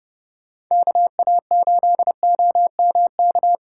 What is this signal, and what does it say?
Morse code, heard as a keyed tone, about 30 wpm, 700 Hz. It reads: KA8OMK